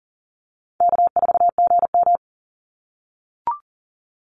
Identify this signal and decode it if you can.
Morse code, heard as a keyed tone, about 40 wpm, 700 Hz. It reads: K4GM
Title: Call Signs - 40wpm